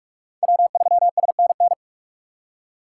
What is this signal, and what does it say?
Morse code, heard as a keyed tone, about 44 wpm, 700 Hz. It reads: W3SNN